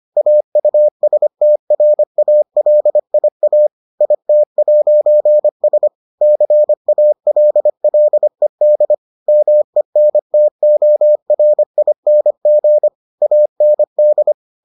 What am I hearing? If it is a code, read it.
Morse code, heard as a keyed tone, about 25 wpm, 600 Hz. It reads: AUSTRALIA IT'S CALLED MENTORING AND